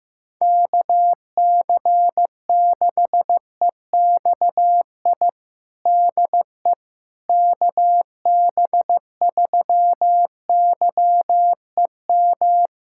Morse code, heard as a keyed tone, about 15 wpm, 700 Hz. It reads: KC6EXI DE KB3YEM
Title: USA Callsign DE Callsign 15wpm